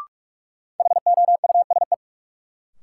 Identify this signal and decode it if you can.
Morse code, heard as a keyed tone, about 46 wpm, 700 Hz. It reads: HOUSE
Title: Sentences from Top 200 Words - New Word Drill - 45wpm